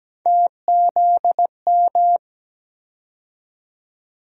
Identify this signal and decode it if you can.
Morse code, heard as a keyed tone, about 17 wpm, 700 Hz. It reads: TZM
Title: USA Call Sign Postfixes - 3 Letters 17wpm